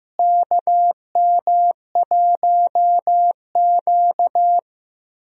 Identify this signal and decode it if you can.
Morse code, heard as a keyed tone, about 16 wpm, 700 Hz. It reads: KM1Q
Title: USA Call Signs - 4 Characters - Repeated 3x - 15wpm